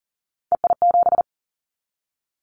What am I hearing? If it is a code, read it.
Morse code, heard as a keyed tone, about 38 wpm, 700 Hz. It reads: EI7
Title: Callsign Prefixes for Top 50 DXCC - 40wpm